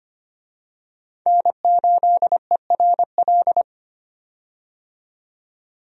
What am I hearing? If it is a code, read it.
Morse code, heard as a keyed tone, about 25 wpm, 700 Hz. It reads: N8ERL
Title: USA Call Signs - 5 Characters - 25wpm